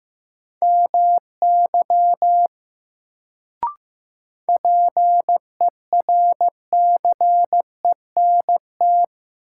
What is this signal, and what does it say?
Morse code, heard as a keyed tone, about 15 wpm, 700 Hz. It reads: MY PERCENT
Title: Sets of 2 Words - New Word Drill - 15wpm